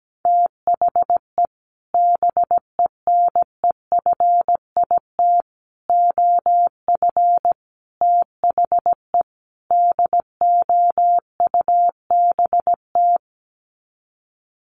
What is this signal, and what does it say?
Morse code, heard as a keyed tone, about 17 wpm, 700 Hz. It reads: THE BENEFIT OF THE DOUBT
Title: Sets of 5 Words Encore 17wpm